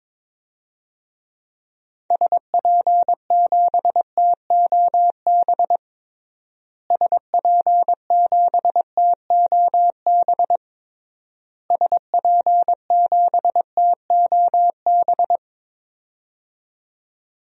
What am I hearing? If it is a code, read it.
Morse code, heard as a keyed tone, about 22 wpm, 700 Hz. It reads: SP7TOB SP7TOB SP7TOB
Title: DXCC-100 Call Signs - 6 Characters - Repeated 3x - 22wpm